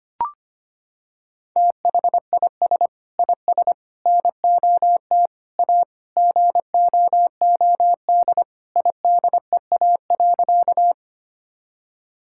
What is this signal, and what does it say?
Morse code, heard as a keyed tone, about 25 wpm, 700 Hz. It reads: THIS IS NOT A GOOD IDEA.
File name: practice / 25wpm / Sentences with 4-Word Phrases - 25wpm